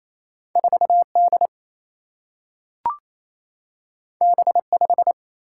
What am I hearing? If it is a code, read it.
Morse code, heard as a keyed tone, about 28 wpm, 700 Hz. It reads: 4 D B 5
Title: Single-Letter Single-Number 28wpm